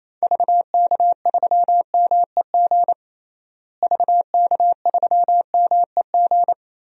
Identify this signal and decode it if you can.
Morse code, heard as a keyed tone, about 28 wpm, 700 Hz. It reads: VK3MEG VK3MEG
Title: DXCC-100 Call Signs - 6 Characters - Repeated 3x - 28wpm